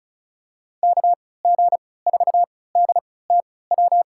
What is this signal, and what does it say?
Morse code, heard as a keyed tone, about 34 wpm, spaced as for 20 wpm, 690 Hz. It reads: KG4DTW
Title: USA Call Signs - 3x Character Spacing - 35wpm